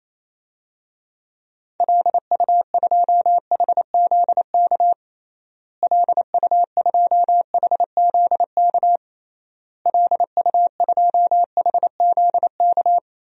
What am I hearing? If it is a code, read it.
Morse code, heard as a keyed tone, about 28 wpm, 700 Hz. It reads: LU2HZK LU2HZK LU2HZK
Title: DXCC-100 Call Signs - 6 Characters - Repeated 3x - 28wpm